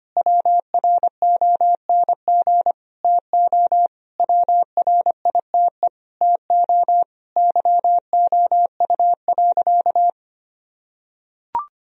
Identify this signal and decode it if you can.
Morse code, heard as a keyed tone, about 25 wpm, 700 Hz. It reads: WRONG TO WRITE TO YOU.
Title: Sentences from Top 1000 Words - Review Part 1 - 25wpm